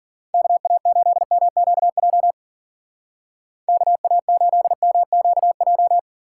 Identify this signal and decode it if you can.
Morse code, heard as a keyed tone, about 51 wpm, 700 Hz. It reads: KA8MQJ KA8MQJ